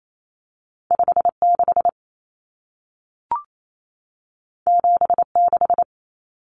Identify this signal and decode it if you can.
Morse code, heard as a keyed tone, about 28 wpm, 700 Hz. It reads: 56 76
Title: Common Cut Numbers - 2-Digit 28wpm